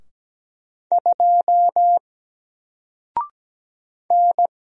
Morse code, heard as a keyed tone, about 17 wpm, 700 Hz. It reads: 2 N
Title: Single Character (Letter:Number:Punctuation:Prosign) 17wpm